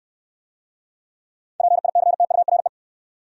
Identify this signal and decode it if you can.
Morse code, heard as a keyed tone, about 65 wpm, 700 Hz. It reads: PICTURE